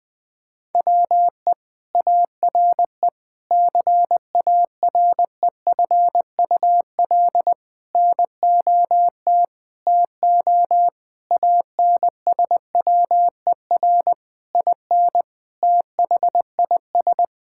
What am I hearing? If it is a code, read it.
Morse code, heard as a keyed tone, about 21 wpm, 700 Hz. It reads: WE ARE CAREFUL NOT TO ANSWER IN THIS
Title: Sentences from Top 1600 Words 20wpm